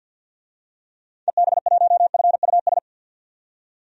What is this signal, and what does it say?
Morse code, heard as a keyed tone, about 46 wpm, 700 Hz. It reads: EB1FUS
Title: DXCC-100 Call Signs - 6 Characters - 50wpm